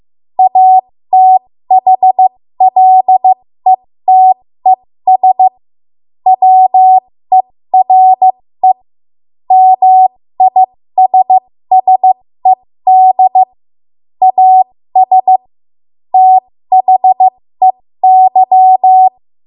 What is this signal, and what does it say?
Morse code, heard as a keyed tone, about 15 wpm, spaced as for 13 wpm, 750 Hz. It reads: ATHLETES WERE MISSED AS THEY
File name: Morse / 13WPM